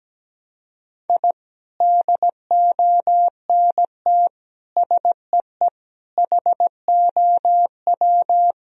Morse code, heard as a keyed tone, about 17 wpm, 700 Hz. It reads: I DONT SEE HOW
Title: Sentences from Top 1000 Words - Review Part 2 - 17wpm